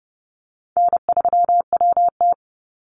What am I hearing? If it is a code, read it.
Morse code, heard as a keyed tone, about 30 wpm, 700 Hz. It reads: N3WT
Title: Easy Call Signs - 30wpm